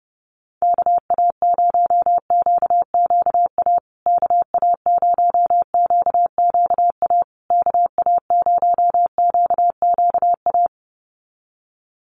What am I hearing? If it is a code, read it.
Morse code, heard as a keyed tone, about 30 wpm, 700 Hz. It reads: KA0QQA KA0QQA KA0QQA
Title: USA Call Signs Repeated 3x 30wpm